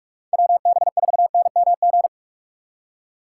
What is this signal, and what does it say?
Morse code, heard as a keyed tone, about 45 wpm, 700 Hz. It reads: WB4NMG